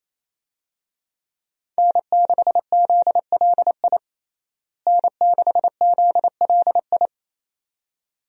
Morse code, heard as a keyed tone, about 28 wpm, 700 Hz. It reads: N6ZLI N6ZLI
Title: USA Call Signs - 5 Characters - Repeated 2x - 28wpm